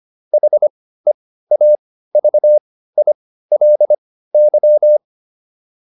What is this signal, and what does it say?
Morse code, heard as a keyed tone, about 25 wpm, spaced as for 15 wpm, 600 Hz. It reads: HEAVILY